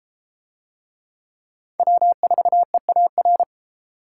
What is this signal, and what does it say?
Morse code, heard as a keyed tone, about 33 wpm, 700 Hz. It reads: W4EAR